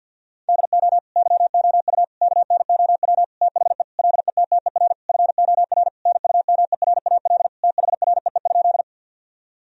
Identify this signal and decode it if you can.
Morse code, heard as a keyed tone, about 50 wpm, 700 Hz. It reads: DO YOU KNOW THE LETTER FOR NUMERAL THREE?
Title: Sentences from Top 1000 Words - Review Part 2 - 50wpm